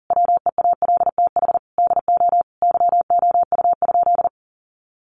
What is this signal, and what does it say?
Morse code, heard as a keyed tone, about 40 wpm, 700 Hz. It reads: WEALTH DO YOU?
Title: Sentences from Top 1200 Words 40wpm